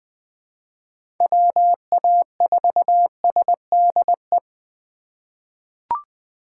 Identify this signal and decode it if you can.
Morse code, heard as a keyed tone, about 20 wpm, 700 Hz. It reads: WA4SDE
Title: USA Call Signs - 6 Characters - 3x Character Spacing - 20wpm